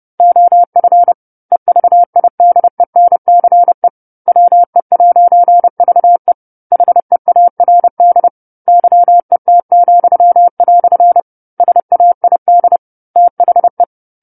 Morse code, heard as a keyed tone, about 30 wpm, 700 Hz. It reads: OF EVIDENCE WE'VE HEARD YET," SAID THE
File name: 10-Chapter-10 30 WPM